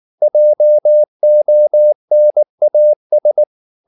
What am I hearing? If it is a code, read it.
Morse code, heard as a keyed tone, about 19 wpm, 600 Hz. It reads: JONAS
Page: Ferie 19 wpm